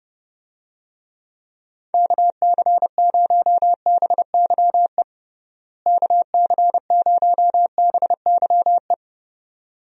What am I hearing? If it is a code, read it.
Morse code, heard as a keyed tone, about 30 wpm, 700 Hz. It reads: KC0BYE KC0BYE
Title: USA Call Signs - 6 Characters - Repeated 2x - 30wpm